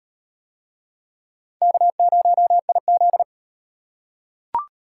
Morse code, heard as a keyed tone, about 33 wpm, 700 Hz. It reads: K0IZ